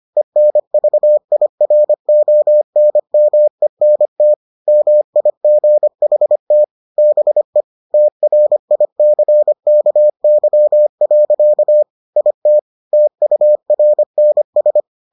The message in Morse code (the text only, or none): ENVIRONMENT MIGHT BE TRICKY. IT TURNS